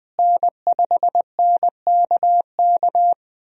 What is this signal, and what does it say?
Morse code, heard as a keyed tone, about 21 wpm, 700 Hz. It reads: N5NKK
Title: USA Call Signs - 5 Characters - Repeated 2x - 20wpm